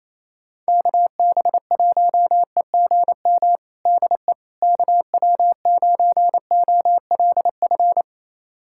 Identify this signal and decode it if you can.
Morse code, heard as a keyed tone, about 28 wpm, 700 Hz. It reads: KB1EGM DE KW9OLF
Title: USA Callsign DE Callsign 28wpm